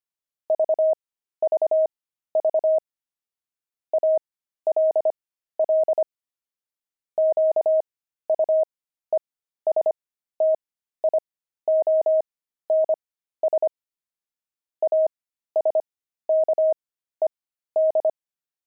Morse code, heard as a keyed tone, about 25 wpm, spaced as for 13 wpm, 630 Hz. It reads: VVV ALL QUESTIONS ASKED
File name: pangram-fs-13-cs-25-watches